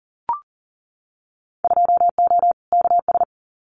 Morse code, heard as a keyed tone, about 40 wpm, 700 Hz. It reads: 2O KS
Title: Winter Field Day Exchanges Encore 40wpm